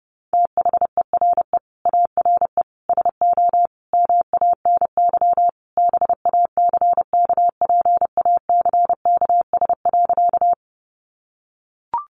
Morse code, heard as a keyed tone, about 30 wpm, 700 Hz. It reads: THERE ARE SO MANY BACKPACKS.
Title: Additional Easy Phrases 30wpm